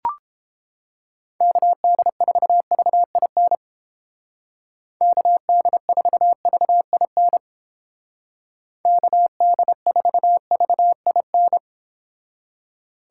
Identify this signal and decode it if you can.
Morse code, heard as a keyed tone, about 29 wpm, 700 Hz. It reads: KD4VIN KD4VIN KD4VIN